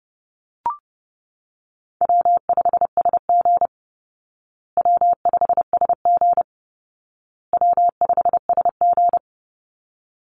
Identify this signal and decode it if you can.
Morse code, heard as a keyed tone, about 30 wpm, 700 Hz. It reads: W5SG W5SG W5SG
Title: Call Signs Repeated 2x - 30wpm